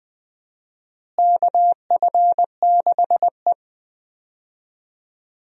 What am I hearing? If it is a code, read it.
Morse code, heard as a keyed tone, about 21 wpm, 700 Hz. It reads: KF6E
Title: USA Call Signs - 4 Characters - Repeated 2x - 20wpm